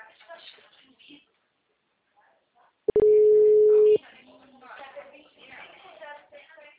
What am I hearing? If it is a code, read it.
Ringback tone (Europe, 425 Hz), with speech.